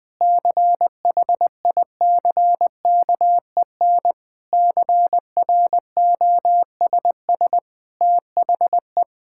Morse code, heard as a keyed tone, about 20 wpm, 700 Hz. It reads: CHICKEN CROSS THE
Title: Sentences from Top 1000 Words - Review Part 5 - 20wpm